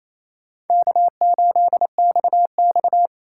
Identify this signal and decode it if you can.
Morse code, heard as a keyed tone, about 28 wpm, 700 Hz. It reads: K8XX
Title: USA Call Signs - 4 Characters - 3x Character Spacing - 28wpm